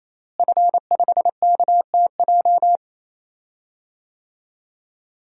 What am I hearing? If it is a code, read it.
Morse code, heard as a keyed tone, about 28 wpm, 700 Hz. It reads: F5KTJ
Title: DXCC-100 Call Signs - 5 Characters - Repeated 3x - 28wpm